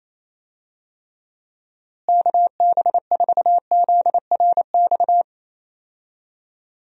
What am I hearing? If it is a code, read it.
Morse code, heard as a keyed tone, about 28 wpm, 700 Hz. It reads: KB4ZRX
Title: USA Call Signs - 6 Characters - Repeated 2x - 28wpm